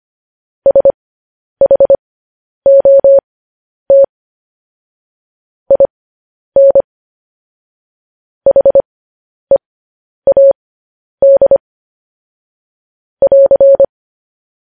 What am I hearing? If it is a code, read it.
Morse code, heard as a keyed tone, about 25 wpm, spaced as for 10 wpm, 550 Hz. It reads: SHOT IN HEAD <AR>